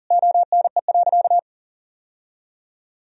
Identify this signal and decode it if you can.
Morse code, heard as a keyed tone, about 40 wpm, 700 Hz. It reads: ONE.
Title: Sentences from Top 1000 Words - Review Part 4 - 40wpm